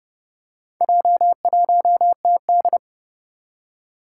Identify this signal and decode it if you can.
Morse code, heard as a keyed tone, about 30 wpm, 700 Hz. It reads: J1TD
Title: Four Challenging Letter-Number Combinations 30wpm